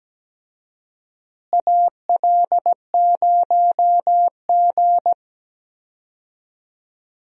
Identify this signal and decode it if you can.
Morse code, heard as a keyed tone, about 17 wpm, 700 Hz. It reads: AL0G
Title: USA Call Signs - 4 Characters - Repeated 2x - 17wpm